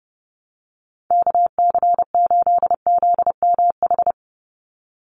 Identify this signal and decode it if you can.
Morse code, heard as a keyed tone, about 30 wpm, 700 Hz. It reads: KC8ZMH
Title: USA Call Signs 30wpm